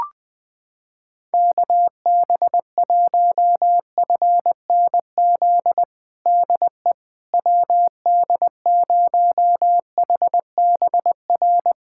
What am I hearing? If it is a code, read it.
Morse code, heard as a keyed tone, about 20 wpm, 700 Hz. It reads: KB1FNZ DE WD0HBR
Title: USA Callsign DE Callsign - 6 Characters - 20wpm